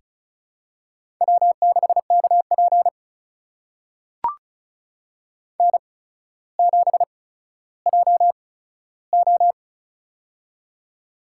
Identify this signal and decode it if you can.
Morse code, heard as a keyed tone, about 36 wpm, spaced as for 15 wpm, 700 Hz. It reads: W6KP N7JO
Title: USA Call Signs - 4 Characters - 8x Character Spacing - 35wpm